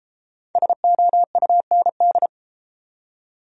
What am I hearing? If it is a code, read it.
Morse code, heard as a keyed tone, about 30 wpm, 700 Hz. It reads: SOUND